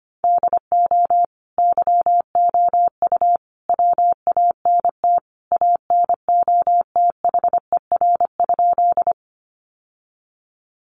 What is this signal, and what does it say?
Morse code, heard as a keyed tone, about 25 wpm, 700 Hz. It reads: DO YOU WANT ANOTHER?
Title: Sentences from Top 600 Words 25wpm